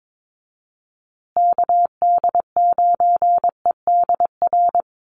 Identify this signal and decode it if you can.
Morse code, heard as a keyed tone, about 22 wpm, 700 Hz. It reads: KD9EDR
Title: USA Call Signs Repeated 2x 22wpm